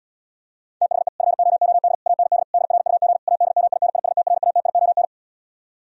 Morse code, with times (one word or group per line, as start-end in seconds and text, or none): 0.8-1.1 THE
1.2-1.9 FOOD
2.1-2.4 AND
2.5-3.2 DRUG
3.3-5.0 ADMINISTRATION